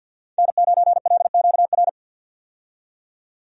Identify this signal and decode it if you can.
Morse code, heard as a keyed tone, about 52 wpm, 700 Hz. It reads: N9LQR
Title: USA Call Signs - 5 Characters - 50wpm